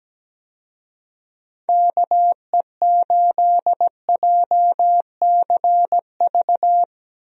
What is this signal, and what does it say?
Morse code, heard as a keyed tone, about 17 wpm, 700 Hz. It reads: KE8JCV
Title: USA Call Signs - 6 Characters - Repeated 2x - 17wpm